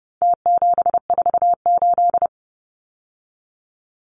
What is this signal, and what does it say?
Morse code, heard as a keyed tone, about 30 wpm, 700 Hz. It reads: T748
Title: Numbers - Contest Style - 4-Digit 30wpm